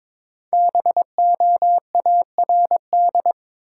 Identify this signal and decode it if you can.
Morse code, heard as a keyed tone, about 22 wpm, 690 Hz. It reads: BOARD
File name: 5 Letter Words - First 3 Letters Given - 22wpm